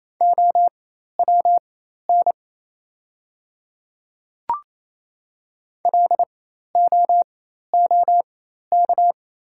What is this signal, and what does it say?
Morse code, heard as a keyed tone, about 27 wpm, spaced as for 13 wpm, 700 Hz. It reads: OWN LOOK
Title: 4x Character Spacing - Top 300 Words - 28wpm